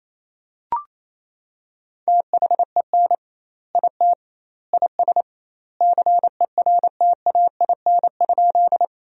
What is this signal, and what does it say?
Morse code, heard as a keyed tone, about 28 wpm, 700 Hz. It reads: THEN IT IS CERTAIN?
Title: 2x Word-Spacing - Sentences from Top 500 Words - 28wpm